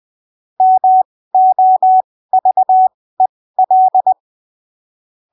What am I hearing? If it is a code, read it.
Morse code, heard as a keyed tone, about 20 wpm, spaced as for 15 wpm, 750 Hz. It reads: MOVEL
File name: arquivos / texto / texto 07 20ppm